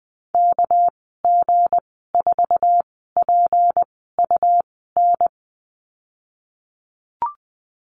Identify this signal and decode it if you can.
Morse code, heard as a keyed tone, about 20 wpm, spaced as for 14 wpm, 700 Hz. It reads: KG4PUN
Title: 219 Lesson 35g - Number 0 - Call Signs 20wpm